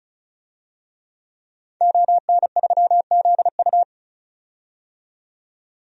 Morse code, heard as a keyed tone, about 36 wpm, 700 Hz. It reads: ON3ZU
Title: DXCC-100 Call Signs - 5 Characters - 35wpm